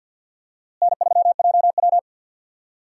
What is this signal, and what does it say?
Morse code, heard as a keyed tone, about 52 wpm, 700 Hz. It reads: N3JW
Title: USA Call Signs - 4 Characters - Repeated 3x - 50wpm